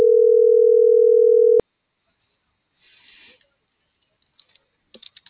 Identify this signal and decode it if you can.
Ringback tone (North America), 440 + 480 Hz.